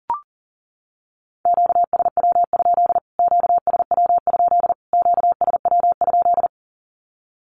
Morse code, heard as a keyed tone, about 40 wpm, 700 Hz. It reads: QSW? QSW? QSW?
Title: Q-Codes Repeated 3x 40wpm